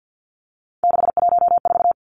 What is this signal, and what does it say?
Morse code, heard as a keyed tone, about 50 wpm, 700 Hz. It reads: 614
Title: Numbers - 3-Digit 50wpm